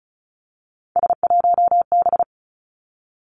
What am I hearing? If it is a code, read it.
Morse code, heard as a keyed tone, about 36 wpm, 700 Hz. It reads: S1B
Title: Three Challenging Letter-Number Combinations 35wpm